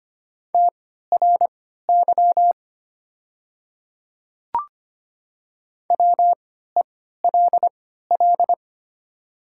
Morse code, heard as a keyed tone, about 25 wpm, spaced as for 14 wpm, 700 Hz. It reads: TRY WELL
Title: 3x Character Spacing - Top 200 Words - 25wpm